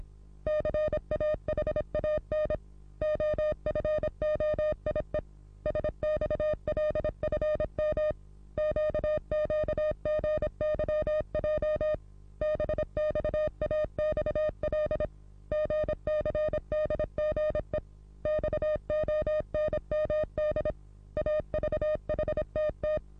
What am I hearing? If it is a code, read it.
Morse code, heard as a keyed tone, about 26 wpm, 610 Hz. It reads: CAHAN OFOIE SXLFM QQGYJ BXAXL GCDGE XONMD AVHTT
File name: repos / Morsekurs / DJ4UF Morsen27